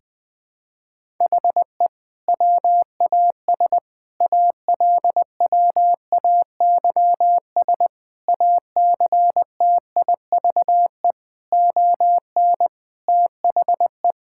Morse code, heard as a keyed tone, about 20 wpm, 700 Hz. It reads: HE WAS ALWAYS ACTIVE ON THE
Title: Sentences from Top 1000 Words - Review Part 6 - 20wpm